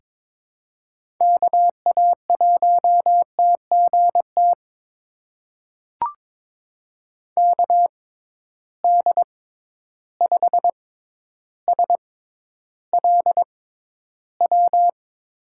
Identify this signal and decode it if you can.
Morse code, heard as a keyed tone, about 22 wpm, spaced as for 11 wpm, 700 Hz. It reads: KA1TGT KD5SLW
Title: USA Call Signs - 6 Characters - 6x Character Spacing - 22wpm